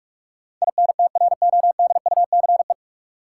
Morse code, heard as a keyed tone, about 41 wpm, 700 Hz. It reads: INTRODUCE
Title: Sentences from Top 1600 Words - New Word Drill - 45wpm